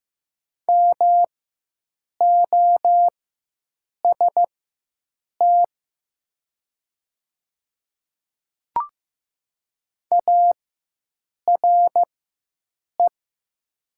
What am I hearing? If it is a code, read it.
Morse code, heard as a keyed tone, about 15 wpm, spaced as for 7 wpm, 700 Hz. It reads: MOST ARE
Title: 4x Character Spacing - Top 100 Words - 15wpm